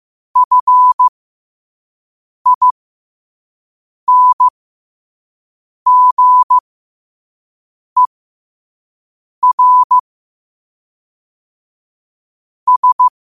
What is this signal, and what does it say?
Morse code, heard as a keyed tone, about 15 wpm, spaced as for 5 wpm, 1000 Hz. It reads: FINGER S
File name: qotdfiles / 5wpm